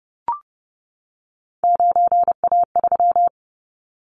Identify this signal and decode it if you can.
Morse code, heard as a keyed tone, about 30 wpm, 700 Hz. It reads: 9A3
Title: Callsign Prefixes for Top 50 DXCC - 30wpm